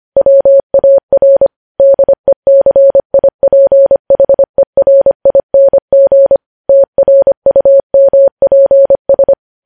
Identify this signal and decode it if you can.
Morse code, heard as a keyed tone, about 25 wpm, 550 Hz. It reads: WAR DECIPHERING TRUMPS
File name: podcast / News headlines-1x-25x25